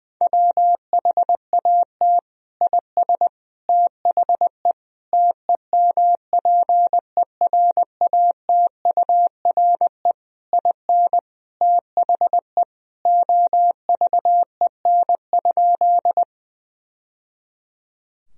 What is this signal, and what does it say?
Morse code, heard as a keyed tone, about 20 wpm, 700 Hz. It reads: WHAT IS THE TEMPERATURE IN THE OVEN?
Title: Sentences from Top 1000 Words - Review Part 4 - 20wpm